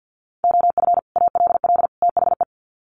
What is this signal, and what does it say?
Morse code, heard as a keyed tone, about 50 wpm, 700 Hz. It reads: OF ALL THE
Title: Sets of 3 Words 50wpm